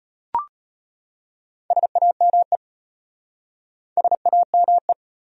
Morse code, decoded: SAME SAME

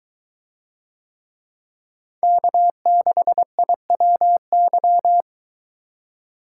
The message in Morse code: K6IWY

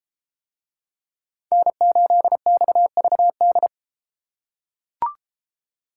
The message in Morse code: N8XVD

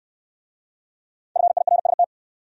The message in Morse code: FIRST

